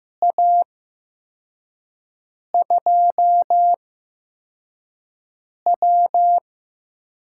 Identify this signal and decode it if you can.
Morse code, heard as a keyed tone, about 15 wpm, spaced as for 4 wpm, 700 Hz. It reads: A2W
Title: USA Call Signs - 6 Characters - 8x Character Spacing - 15wpm